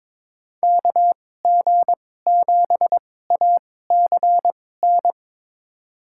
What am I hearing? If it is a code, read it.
Morse code, heard as a keyed tone, about 22 wpm, spaced as for 16 wpm, 700 Hz. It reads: KG7ACN
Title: USA Call Signs - 6 Characters - 2x Character Spacing - 22wpm